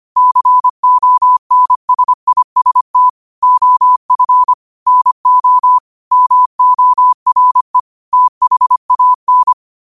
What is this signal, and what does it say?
Morse code, heard as a keyed tone, about 25 wpm, 1000 Hz. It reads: CONSIST OF NO MORE THAN
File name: qotdfiles / 25wpm